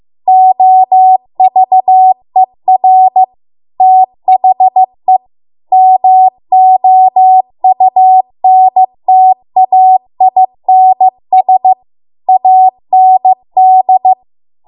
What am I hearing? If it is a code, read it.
Morse code, heard as a keyed tone, about 15 wpm, 750 Hz. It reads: OVER THE MOUNTAINS AND